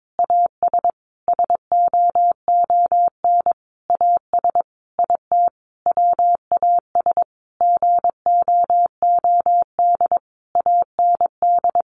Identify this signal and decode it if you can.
Morse code, heard as a keyed tone, about 22 wpm, 700 Hz. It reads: AS SOON AS IT WAS GOOD AND
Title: Sentences from Top 600 Words 22wpm